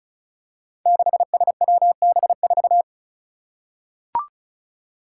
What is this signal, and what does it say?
Morse code, heard as a keyed tone, about 36 wpm, 700 Hz. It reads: 6SWB4
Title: Five Challenging Letter-Number Combinations 35wpm